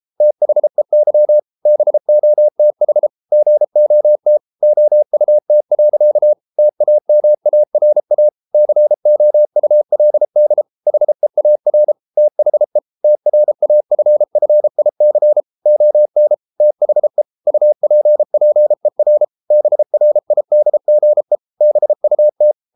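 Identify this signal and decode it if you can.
Morse code, heard as a keyed tone, about 33 wpm, 600 Hz. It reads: THEY BOTH GOT OUT. TAMARA COULD HEAR THE TRAFFIC ON THE UPPER BRIDGE BUT
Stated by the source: Morse code practice at 32 wpm from CW med Gnister, reading Ken Follett's novel Never in English.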